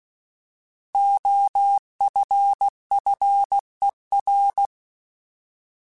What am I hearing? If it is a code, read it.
Morse code, heard as a keyed tone, about 16 wpm, 800 Hz. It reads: OFFER
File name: Alphabetical / Alphabetical mp3s / Os @15